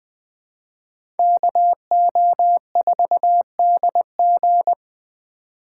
Morse code, heard as a keyed tone, about 24 wpm, 700 Hz. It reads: KO4DG
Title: Speed Racing - USA 2x2 Call Signs 20wpm